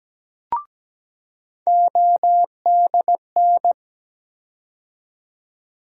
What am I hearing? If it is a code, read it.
Morse code, heard as a keyed tone, about 17 wpm, 700 Hz. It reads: ODN
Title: USA Call Sign Postfixes - 3 Letters 17wpm